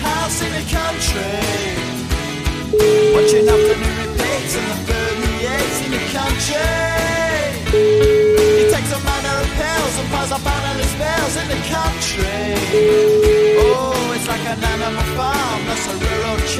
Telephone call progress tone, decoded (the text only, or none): ringback tone